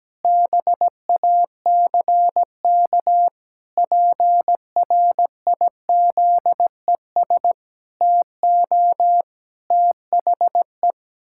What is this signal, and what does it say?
Morse code, heard as a keyed tone, about 17 wpm, 700 Hz. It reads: BACK PRIZES TO THE
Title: Pangram Challenge v2 - 17wpm